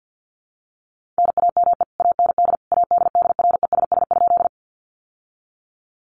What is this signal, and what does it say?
Morse code, heard as a keyed tone, about 50 wpm, 700 Hz. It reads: NAME AND ADDRESS?
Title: Sentences from Top 1000 Words 50wpm